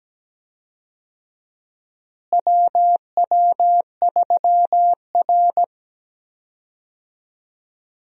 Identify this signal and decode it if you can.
Morse code, heard as a keyed tone, about 17 wpm, 700 Hz. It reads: WW3R
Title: USA Call Signs - 4 Characters - Repeated 2x - 17wpm